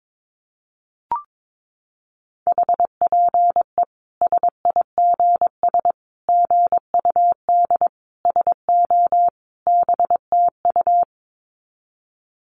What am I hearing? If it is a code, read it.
Morse code, heard as a keyed tone, about 22 wpm, 700 Hz. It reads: HPE SIGS GUD SO BTU
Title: Common QSO Phrases 22wpm